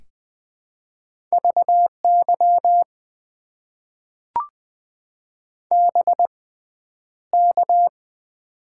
Morse code, heard as a keyed tone, about 20 wpm, spaced as for 10 wpm, 700 Hz. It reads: VY BK
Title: Top 90 QSO Elements - 6x Character Spacing - 20wpm